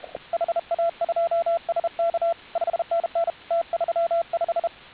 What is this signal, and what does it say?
Morse code, heard as a keyed tone, about 32 wpm, 680 Hz, with radio noise. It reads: HA2SK 5NN T35